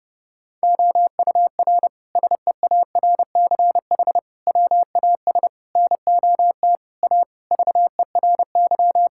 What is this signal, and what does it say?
Morse code, heard as a keyed tone, about 30 wpm, 700 Hz. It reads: OUR SEARCH WAS NOT A VERY
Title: Sentences from Top 1000 Words - Review Part 5 - 30wpm